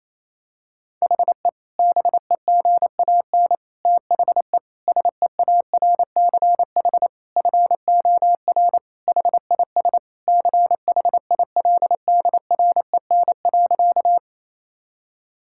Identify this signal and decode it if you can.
Morse code, heard as a keyed tone, about 28 wpm, 700 Hz. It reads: HE BEGAN THE SEARCH FOR HIS CHILDREN.
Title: Sentences from Top 1000 Words - Review Part 5 - 28wpm